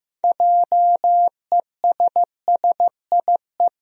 Morse code, heard as a keyed tone, about 15 wpm, 700 Hz. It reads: JESSIE
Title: USA Common Names - 15wpm